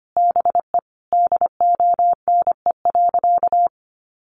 Morse code, heard as a keyed tone, about 25 wpm, 700 Hz. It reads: BE DONE.
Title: Sentences from Top 800 Words 25wpm